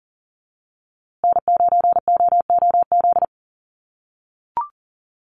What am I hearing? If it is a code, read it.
Morse code, heard as a keyed tone, about 40 wpm, 700 Hz. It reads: N9OOZ
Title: USA Call Signs Repeated 3x 40wpm